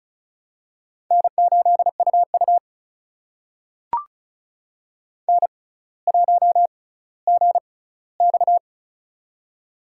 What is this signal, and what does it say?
Morse code, heard as a keyed tone, about 36 wpm, spaced as for 18 wpm, 700 Hz. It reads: N8UU N1GX
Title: USA Call Signs - 4 Characters - 6x Character Spacing - 35wpm